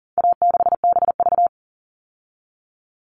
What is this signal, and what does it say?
Morse code, heard as a keyed tone, about 40 wpm, 700 Hz. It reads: A6BV
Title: All Cut Numbers - 4-Digit 40wpm